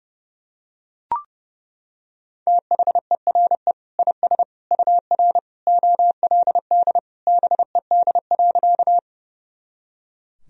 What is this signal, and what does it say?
Morse code, heard as a keyed tone, about 30 wpm, 700 Hz. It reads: THERE IS UR OLD BED.
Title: Sentences from Top 500 Words and Abbreviations 30wpm